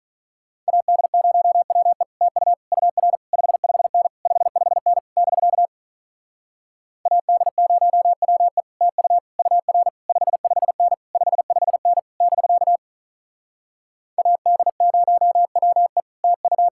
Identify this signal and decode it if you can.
Morse code, heard as a keyed tone, about 48 wpm, 700 Hz. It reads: AD0WE TU UR 55N 55N <BK> AD0WE TU UR 55N 55N <BK> AD0WE TU